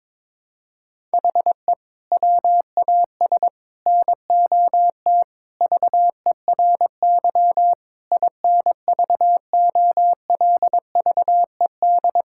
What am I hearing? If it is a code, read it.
Morse code, heard as a keyed tone, about 22 wpm, 700 Hz. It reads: HE WAS NOT VERY INVOLVED
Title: Sentences from Top 1000 Words - Review Part 5 - 22wpm